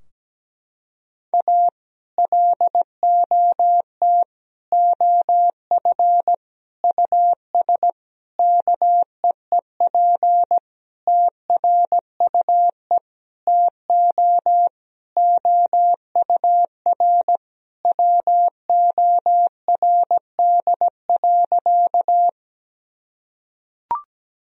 Morse code, heard as a keyed tone, about 17 wpm, 700 Hz. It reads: A LOT OF US KEEP TRUE TO OUR WORD.
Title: Sentences from Top 1000 Words - Review Part 3 - 17wpm